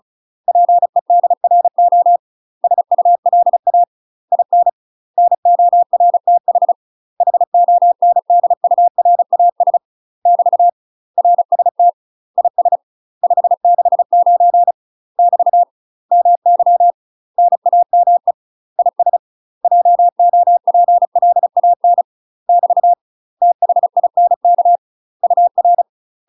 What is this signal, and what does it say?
Morse code, heard as a keyed tone, about 34 wpm, 700 Hz. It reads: PEDRO SULA IN NORTH HONDURAS <BT> RST IS 569 <BT> MY NAME IS JOPLAN <BT> THINK UR
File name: UK MASTER QSOs TRAINING / 35WPM / UK MASTER QSOs 0043